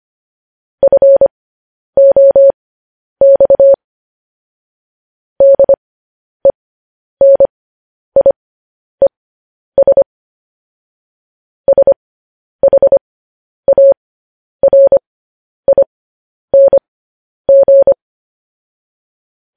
Morse code, heard as a keyed tone, about 25 wpm, spaced as for 10 wpm, 550 Hz. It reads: FOX DENIES SHARING
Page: News Headlines in Morse Code at 10 WPM